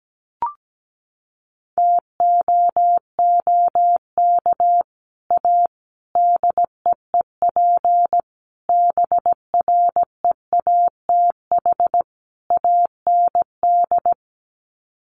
Morse code, heard as a keyed tone, about 17 wpm, 690 Hz. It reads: TOOK A DEEP BREATH AND
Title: Sets of 5 Words Encore 17wpm